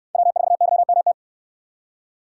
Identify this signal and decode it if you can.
Morse code, heard as a keyed tone, about 68 wpm, 700 Hz. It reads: W4YMT